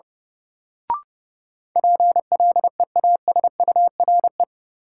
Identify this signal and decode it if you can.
Morse code, heard as a keyed tone, about 30 wpm, 700 Hz. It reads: PLEASURE
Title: Sentences from Top 1600 Words - New Word Drill Alt - 30wpm